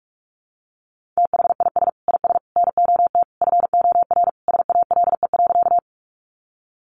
Morse code, heard as a keyed tone, about 44 wpm, 700 Hz. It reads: THIS IS NOT FOR SALE.
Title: Sentences from Top 1400 Words 45wpm